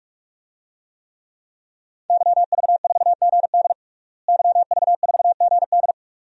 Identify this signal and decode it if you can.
Morse code, heard as a keyed tone, about 46 wpm, 700 Hz. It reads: YV4GD YV4GD
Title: DXCC-100 Call Signs - 5 Characters - Repeated 2x - 45wpm